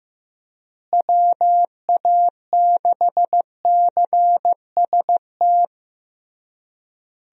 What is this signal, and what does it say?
Morse code, heard as a keyed tone, about 15 wpm, 700 Hz. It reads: WA6CST
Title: USA Call Signs - 6 Characters - Repeated 3x - 15wpm